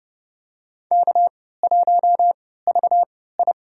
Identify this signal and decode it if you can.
Morse code, heard as a keyed tone, about 32 wpm, spaced as for 17 wpm, 700 Hz. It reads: K1VI
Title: USA Call Signs - 3x Character Spacing - 30wpm